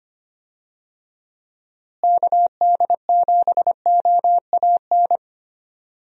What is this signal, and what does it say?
Morse code, heard as a keyed tone, about 25 wpm, 700 Hz. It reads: KD7OAN
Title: USA Call Signs - 6 Characters - Repeated 3x - 25wpm